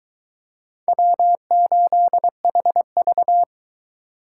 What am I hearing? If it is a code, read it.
Morse code, heard as a keyed tone, about 21 wpm, 700 Hz. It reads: W8HV